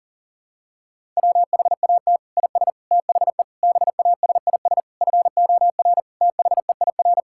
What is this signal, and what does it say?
Morse code, heard as a keyed tone, about 40 wpm, 700 Hz. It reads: WHAT IS THE BASIS FOR THEIR
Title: Sentences from Top 1000 Words - Review Part 6 - 40wpm